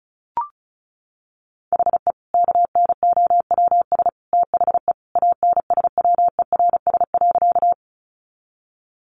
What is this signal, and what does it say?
Morse code, heard as a keyed tone, about 34 wpm, 700 Hz. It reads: HE KNOWS THE ANSWERS.
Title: Additional Easy Phrases 35wpm